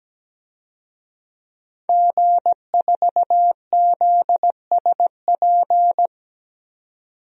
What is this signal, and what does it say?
Morse code, heard as a keyed tone, about 17 wpm, 700 Hz. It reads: G4ZSP
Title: DXCC-100 Call Signs - 5 Characters - 17wpm